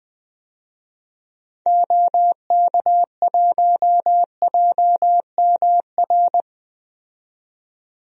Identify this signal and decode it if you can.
Morse code, heard as a keyed tone, about 20 wpm, 700 Hz. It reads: OK1JMR
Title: DXCC-100 Call Signs - 6 Characters - 20wpm